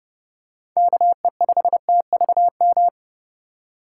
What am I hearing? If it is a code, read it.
Morse code, heard as a keyed tone, about 30 wpm, 700 Hz. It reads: KE5TVM
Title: USA Call Signs - 4x Character Spacing - 30wpm